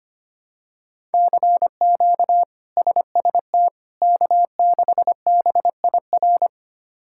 Morse code, heard as a keyed tone, about 25 wpm, 700 Hz. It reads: CQ SST K6BIR
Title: K1USN SST QSO Practice - 25wpm